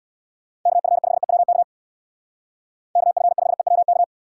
Morse code, heard as a keyed tone, about 59 wpm, 700 Hz. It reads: KF5EWK KF5EWK
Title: Speed Racing - USA 2x3 Call Signs 50wpm